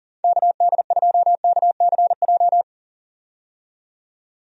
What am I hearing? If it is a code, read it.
Morse code, heard as a keyed tone, about 40 wpm, 690 Hz. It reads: KD2KCJ
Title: USA Call Signs - 6 Characters - 40wpm